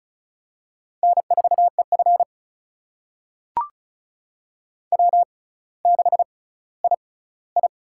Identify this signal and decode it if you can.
Morse code, heard as a keyed tone, about 36 wpm, spaced as for 18 wpm, 700 Hz. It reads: N4EF W6II
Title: USA Call Signs - 4 Characters - 6x Character Spacing - 35wpm